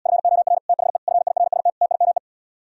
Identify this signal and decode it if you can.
Morse code, heard as a keyed tone, about 77 wpm, 700 Hz. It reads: FOR THE FIRST TIME